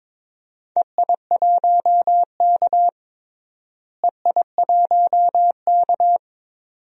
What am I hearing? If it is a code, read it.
Morse code, heard as a keyed tone, about 22 wpm, 700 Hz. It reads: EI1K EI1K
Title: DXCC-100 Call Signs - 4 Characters - Repeated 2x - 22wpm